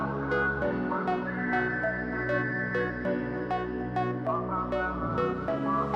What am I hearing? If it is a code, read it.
There is no code.